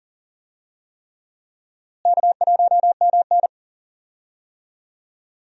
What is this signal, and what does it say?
Morse code, heard as a keyed tone, about 40 wpm, 700 Hz. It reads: K1MN
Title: USA Call Signs - 4 Characters - Repeated 3x - 40wpm